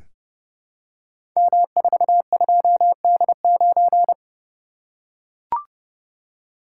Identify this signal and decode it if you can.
Morse code, heard as a keyed tone, about 30 wpm, 700 Hz. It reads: M42D9
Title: Code Groups - 5 Letter:Numbers - 30wpm